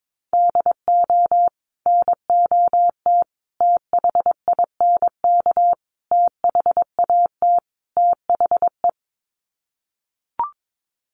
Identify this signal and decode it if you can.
Morse code, heard as a keyed tone, about 22 wpm, 700 Hz. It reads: DO NOT THINK THAT THE
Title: Sets of 5 Words Encore2 22wpm